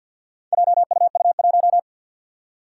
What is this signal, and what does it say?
Morse code, heard as a keyed tone, about 47 wpm, 700 Hz. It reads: JUU1
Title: Four Challenging Letter-Number Combinations 50wpm